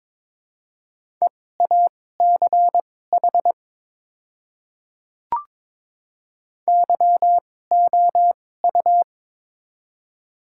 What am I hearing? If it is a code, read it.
Morse code, heard as a keyed tone, about 22 wpm, spaced as for 16 wpm, 700 Hz. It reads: EACH YOU
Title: 2x Character Spacing - Top 100 Words - 22wpm